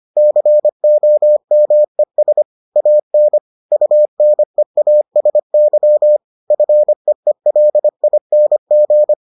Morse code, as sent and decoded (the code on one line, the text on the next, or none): -.-. --- -- . ... / .- -. / ..- -. .- ... -.-- / ..-. . .-.. .. -. --.
COMES AN UNEASY FEELING